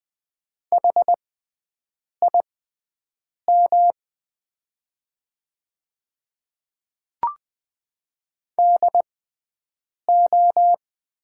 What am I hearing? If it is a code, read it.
Morse code, heard as a keyed tone, about 20 wpm, spaced as for 7 wpm, 700 Hz. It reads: HIM DO
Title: 6x Character Spacing - Top 200 Words - 20wpm